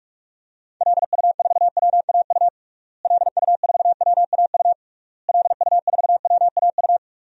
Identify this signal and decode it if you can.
Morse code, heard as a keyed tone, about 46 wpm, 700 Hz. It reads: LU4WAU LU4WAU LU4WAU
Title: DXCC-100 Call Signs - 6 Characters - Repeated 3x - 45wpm